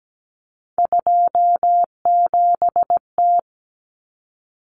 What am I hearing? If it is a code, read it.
Morse code, heard as a keyed tone, about 17 wpm, 700 Hz. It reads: 27T
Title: Common Cut Numbers - 3-Digit 17wpm